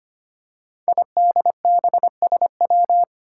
Morse code, heard as a keyed tone, about 25 wpm, 700 Hz. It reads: IDBSW
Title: Five Challenging Letter-Number Combinations 25wpm